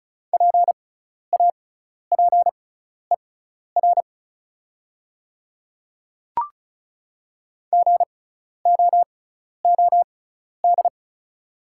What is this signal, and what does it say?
Morse code, heard as a keyed tone, about 36 wpm, spaced as for 12 wpm, 690 Hz. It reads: PAPER GOOD
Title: 6x Character Spacing - Top 300 Words - 35wpm